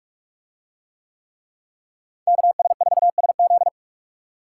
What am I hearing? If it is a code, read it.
Morse code, heard as a keyed tone, about 43 wpm, 700 Hz. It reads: KS4SZ